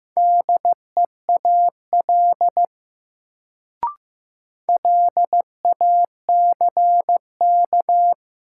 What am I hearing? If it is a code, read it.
Morse code, heard as a keyed tone, about 15 wpm, 700 Hz. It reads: DEAL LACK
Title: Sets of 3 Words - New Word Drill - 15wpm